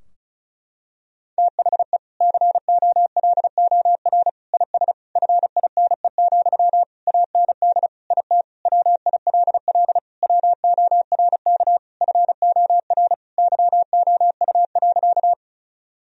Morse code, heard as a keyed tone, about 34 wpm, 700 Hz. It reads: THE COLOR IS FINE, AND IT WILL WORK FOR YOU.
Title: Sentences from Top 1000 Words - Review Part 2 - 35wpm